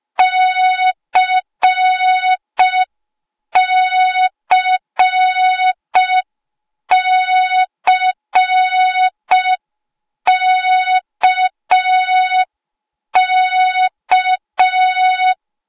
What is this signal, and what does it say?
Morse code, heard as a keyed tone, about 5 wpm, 3700 Hz. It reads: CCCKK